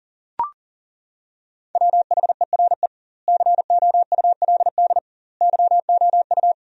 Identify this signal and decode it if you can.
Morse code, heard as a keyed tone, about 40 wpm, 700 Hz. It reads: WHERE COULD YOU
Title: 2x Word-Spacing - Sentences from Top 300 Words - 40wpm